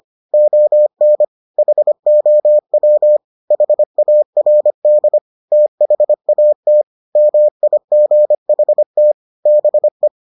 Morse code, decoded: ON HOW HARD THAT MIGHT BE

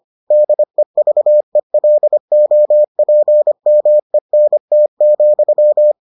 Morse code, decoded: DEVELOPMENT,

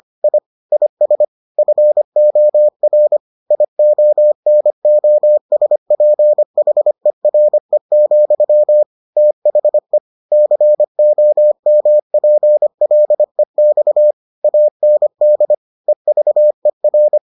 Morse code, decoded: I IS FOR IONOSPHERE, THE COMPLEX AND EVER

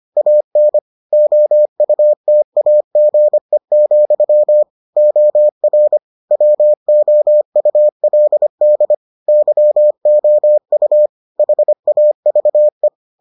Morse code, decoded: AN OUTAGE, OR WOULD YOU HAVE

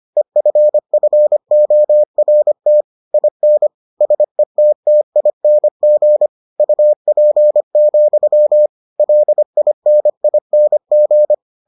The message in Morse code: EFFORT IN SETTING UP, LINING

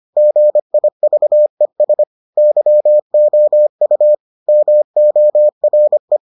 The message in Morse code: GIVES YOU MORE

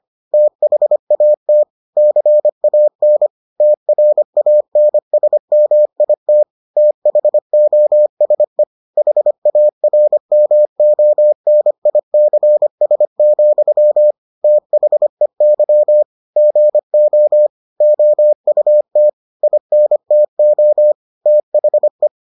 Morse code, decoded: THAT CAN TRANSMIT THOSE HARMONICS, THEY GO OUT INTO THE

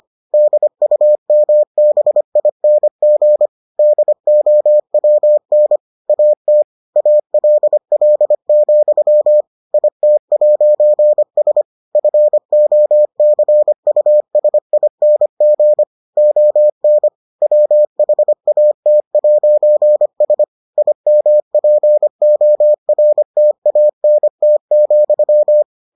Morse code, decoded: DUMBING DOWN AT ALL, IT'S FOCUSING ON WHAT'S IMPORTANT,

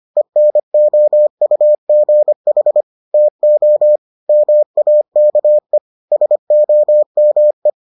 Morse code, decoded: ENOUGH TO MAKE SOME

